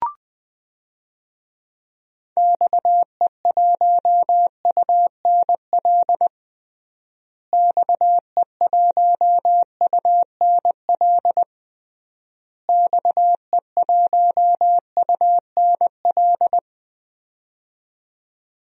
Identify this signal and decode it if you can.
Morse code, heard as a keyed tone, about 20 wpm, 700 Hz. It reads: XE1UNL XE1UNL XE1UNL
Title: DXCC-100 Call Signs - 6 Characters - Repeated 3x - 20wpm